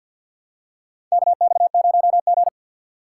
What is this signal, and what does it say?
Morse code, heard as a keyed tone, about 46 wpm, 690 Hz. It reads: KX0G
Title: USA Call Signs - 4 Characters - 6x Character Spacing - 50wpm